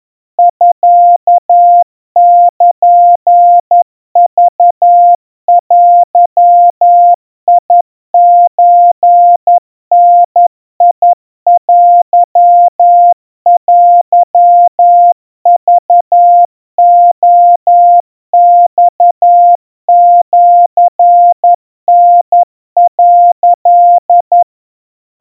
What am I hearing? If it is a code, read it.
Morse code (Japanese Wabun code), heard as a keyed tone, about 11 wpm, 700 Hz. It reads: ミルクデソダテテクレマシタ」
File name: kawa / Morse sample / kamyu 40 cpm 700 Hz Crispy